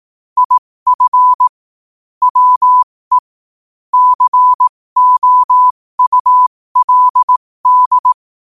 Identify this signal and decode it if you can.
Morse code, heard as a keyed tone, about 18 wpm, spaced as for 15 wpm, 1000 Hz. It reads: IF WE COULD
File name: qotdfiles / 15wpm